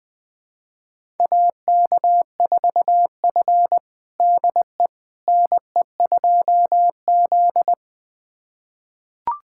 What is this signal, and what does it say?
Morse code, heard as a keyed tone, about 20 wpm, 700 Hz. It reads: AK4F DE NE2Z
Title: USA Callsign DE Callsign - 4 Characters - 20wpm